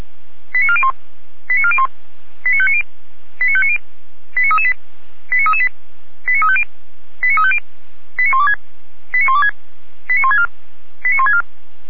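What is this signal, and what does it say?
Five-tone selective call (ZVEI), heard as quick runs of five tones, tones of about 70 ms. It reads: 89401 89401 89590 89590 89338 89338 89350 89350 89126 89126 89164 89164
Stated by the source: ZVEI five-tone call 89700 received on an SDR.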